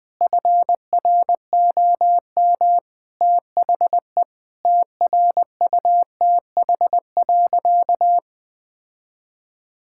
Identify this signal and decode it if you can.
Morse code, heard as a keyed tone, about 20 wpm, 700 Hz. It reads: FROM THE TRUTH.
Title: Sentences with 7-Word Phrases - 20wpm